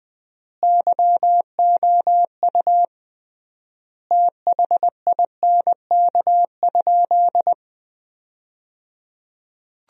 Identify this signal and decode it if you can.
Morse code, heard as a keyed tone, about 21 wpm, 700 Hz. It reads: YOU THINK?
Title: 3x Word-Spacing - Sentences from Top 200 Words - 20wpm